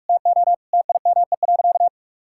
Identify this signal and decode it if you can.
Morse code, heard as a keyed tone, about 46 wpm, 700 Hz. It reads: TO TIME.
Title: Sentences from Top 1000 Words - Review Part 6 - 45wpm